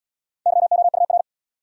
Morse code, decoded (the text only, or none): YOUR